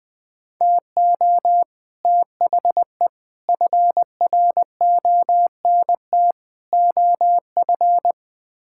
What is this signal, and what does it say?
Morse code, heard as a keyed tone, about 20 wpm, 700 Hz. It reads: TO THE FRONT OF
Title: Build A Phrase - 5-Word Phrases - 20wpm